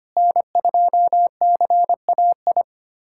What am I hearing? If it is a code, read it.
Morse code, heard as a keyed tone, about 25 wpm, 700 Hz. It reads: N2CAI
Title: USA Callsign DE Callsign - 5 Characters - 25wpm